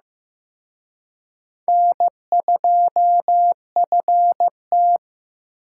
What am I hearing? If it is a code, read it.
Morse code, heard as a keyed tone, about 18 wpm, 700 Hz. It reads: N2FT
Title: Speed Racing - USA 1x2 Call Signs 15wpm